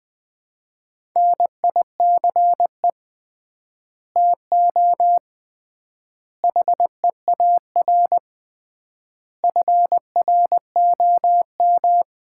Morse code, heard as a keyed tone, about 20 wpm, 700 Hz. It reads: NICE TO HEAR FROM
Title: Common QSO Phrases - 3x Word-Spacing - 20wpm